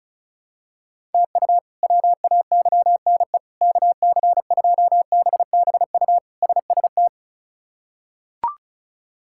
Morse code, heard as a keyed tone, about 34 wpm, 700 Hz. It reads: TU WAYNE KC2BBU SST
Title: K1USN SST QSO Practice - 35wpm